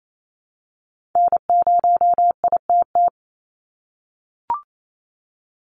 Morse code, heard as a keyed tone, about 28 wpm, 700 Hz. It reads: N0ITT
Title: USA Call Signs 28wpm